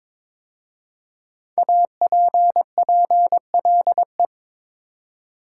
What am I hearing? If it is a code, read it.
Morse code, heard as a keyed tone, about 22 wpm, 700 Hz. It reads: APPLE
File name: 5 Letter Words - First 3 Letters Given - 22wpm